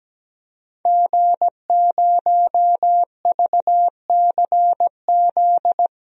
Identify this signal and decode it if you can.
Morse code, heard as a keyed tone, about 17 wpm, 700 Hz. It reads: G0VCZ
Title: DXCC-100 Call Signs - 5 Characters - 17wpm